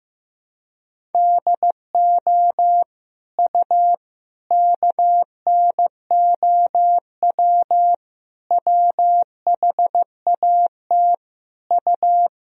Morse code, heard as a keyed tone, about 15 wpm, 700 Hz. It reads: DO U KNOW WHAT U
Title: Sentences from Top 100 Words and Abbreviations 15wpm